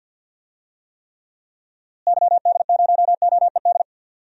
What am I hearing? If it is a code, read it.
Morse code, heard as a keyed tone, about 51 wpm, 700 Hz. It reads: YD0OED
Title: DXCC-100 Call Signs - 6 Characters - Repeated 3x - 50wpm